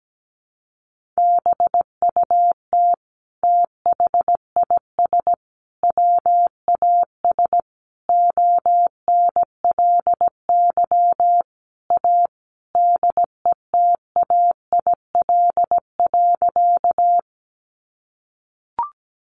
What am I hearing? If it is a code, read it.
Morse code, heard as a keyed tone, about 17 wpm, 700 Hz. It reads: BUT THIS WAS ONLY A DETAIL.
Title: Sentences from Top 1500 Words 17wpm